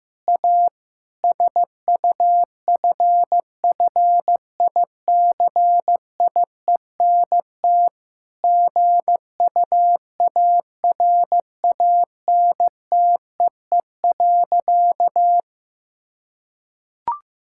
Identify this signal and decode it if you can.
Morse code, heard as a keyed tone, about 15 wpm, 700 Hz. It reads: A SUFFICIENT GUARANTEE.
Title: Sentences from Top 1600 Words 15wpm